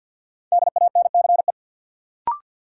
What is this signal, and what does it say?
Morse code, heard as a keyed tone, about 53 wpm, 700 Hz. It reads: DANCE
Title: Sentences from Top 700 Words - New Word Drill - 50wpm